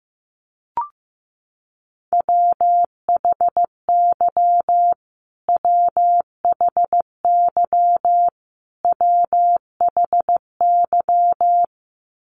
Morse code, heard as a keyed tone, about 15 wpm, 700 Hz. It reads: WHY WHY WHY
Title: Top 200 Words Repeated 15wpm